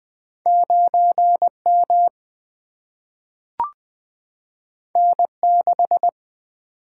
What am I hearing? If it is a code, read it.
Morse code, heard as a keyed tone, about 20 wpm, 700 Hz. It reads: 9 M N 6
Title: Single-Letter Single-Number 20wpm